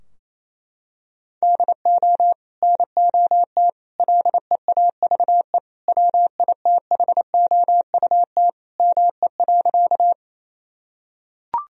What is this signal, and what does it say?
Morse code, heard as a keyed tone, about 28 wpm, 700 Hz. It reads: DO NOT LEAVE WITHOUT ME.
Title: Sentences from Top 1000 Words - Review Part 3 - 28wpm